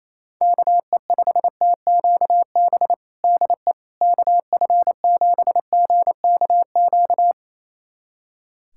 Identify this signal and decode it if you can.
Morse code, heard as a keyed tone, about 27 wpm, 700 Hz. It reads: KE5TQB DE KF7GKQ
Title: USA Callsign DE Callsign - 6 Characters - 28wpm